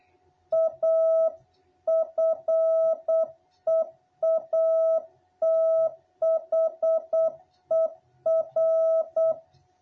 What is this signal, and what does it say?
Morse code, heard as a keyed tone, about 8 wpm, 660 Hz. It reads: A FEATHER
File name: jh8cbh / 2021cw